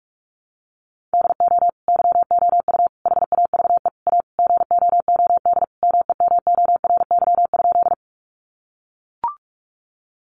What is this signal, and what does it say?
Morse code, heard as a keyed tone, about 44 wpm, 700 Hz. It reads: DO YOU HAVE A GOOD MEMORY?
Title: Sentences from Top 1300 Words 45wpm